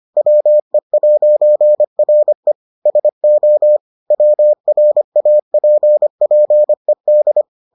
Morse code, heard as a keyed tone, about 25 wpm, 600 Hz. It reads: WE'RE SO WRAPPED